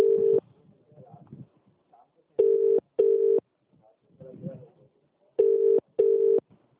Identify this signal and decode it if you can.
Ringback tone (UK), 400 + 450 Hz.